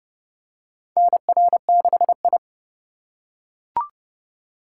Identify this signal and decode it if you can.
Morse code, heard as a keyed tone, about 32 wpm, 700 Hz. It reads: NR6I
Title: USA Call Signs - 4 Characters - 4x Character Spacing - 30wpm